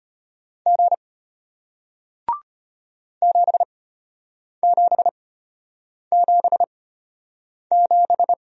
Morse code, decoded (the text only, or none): G 7777